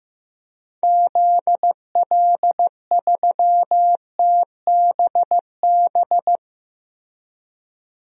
Morse code, heard as a keyed tone, about 15 wpm, 700 Hz. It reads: ZL3TBB
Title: DXCC-100 Call Signs - 6 Characters - 15wpm